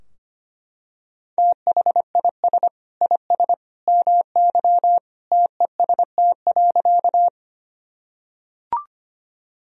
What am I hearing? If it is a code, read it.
Morse code, heard as a keyed tone, about 25 wpm, 700 Hz. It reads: THIS IS MY TEST.
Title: Sentences from Top 1000 Words - Review Part 3 - 25wpm